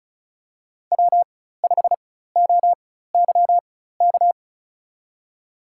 Morse code, heard as a keyed tone, about 36 wpm, spaced as for 16 wpm, 700 Hz. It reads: W5OYK
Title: USA Call Signs - 4x Character Spacing - 35wpm